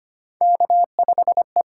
Morse code, heard as a keyed tone, about 25 wpm, 700 Hz. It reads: K5E
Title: USA Call Signs - 5 Characters - 8x Character Spacing - 25wpm